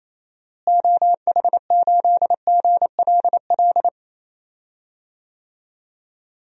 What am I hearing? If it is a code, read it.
Morse code, heard as a keyed tone, about 28 wpm, 700 Hz. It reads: OH8GLL
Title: DXCC-100 Call Signs - 6 Characters - Repeated 3x - 28wpm